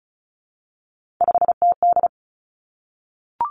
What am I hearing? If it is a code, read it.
Morse code, heard as a keyed tone, about 36 wpm, 700 Hz. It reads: 5TD
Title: Three Challenging Letter-Number Combinations 35wpm